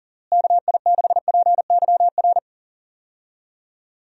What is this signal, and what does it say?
Morse code, heard as a keyed tone, about 37 wpm, 700 Hz. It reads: KI6PYR
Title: USA Call Signs - 6 Characters - Repeated 2x - 40wpm